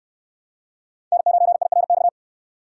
Morse code, heard as a keyed tone, about 60 wpm, 700 Hz. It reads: N9IRX